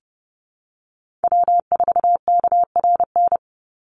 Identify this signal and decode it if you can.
Morse code, heard as a keyed tone, about 30 wpm, 700 Hz. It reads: W4KRN
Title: NA SOTA Chaser Callsigns 30wpm